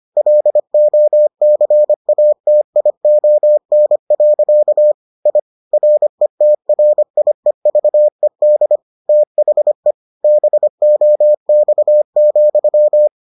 Morse code, heard as a keyed tone, about 25 wpm, 600 Hz. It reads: LOCATION. I RETRIEVED THE BOX,